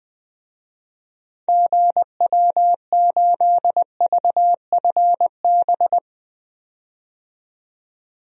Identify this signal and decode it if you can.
Morse code, heard as a keyed tone, about 21 wpm, 700 Hz. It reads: GW8VFB
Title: DXCC-100 Call Signs - 6 Characters - 20wpm